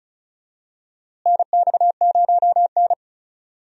Morse code, heard as a keyed tone, about 36 wpm, 700 Hz. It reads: NX0N
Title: USA Call Signs - 4 Characters - Repeated 2x - 35wpm